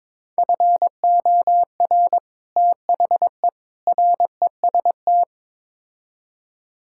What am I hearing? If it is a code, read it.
Morse code, heard as a keyed tone, about 22 wpm, 700 Hz. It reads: FOR THE REST
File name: practice / 22wpm / Build A Phrase - 5-Word Phrases - 22wpm